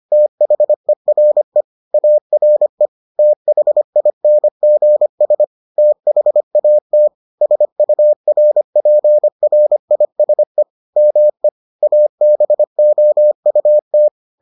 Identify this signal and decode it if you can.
Morse code, heard as a keyed tone, about 25 wpm, 600 Hz. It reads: THERE ARE THINGS THAT SURPRISE ME ABOUT